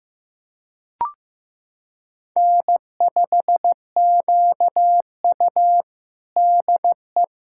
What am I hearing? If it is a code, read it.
Morse code, heard as a keyed tone, about 15 wpm, 700 Hz. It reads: N5QU DE
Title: USA Callsign DE Callsign - 4 Characters - 15wpm